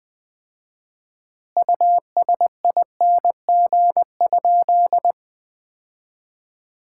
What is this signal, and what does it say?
Morse code, heard as a keyed tone, about 20 wpm, 700 Hz. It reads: USING?
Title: Common QSO Phrases - 4x Word-Spacing - 20wpm